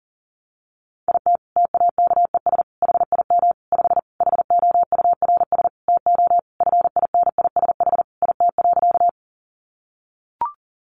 Morse code, heard as a keyed tone, about 40 wpm, 700 Hz. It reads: IT TAKES HIM 5 HOURS TO FINISH IT.
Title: Additional Easy Phrases 40wpm